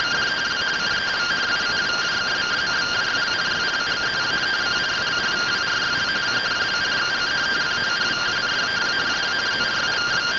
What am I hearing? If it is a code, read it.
Two-tone teleprinter signal, about 170 Hz shift.